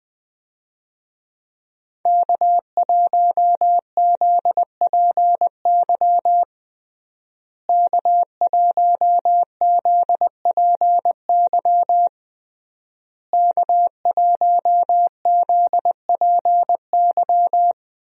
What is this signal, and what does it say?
Morse code, heard as a keyed tone, about 20 wpm, 700 Hz. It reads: K1ZPY K1ZPY K1ZPY
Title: USA Call Signs - 5 Characters - Repeated 3x - 20wpm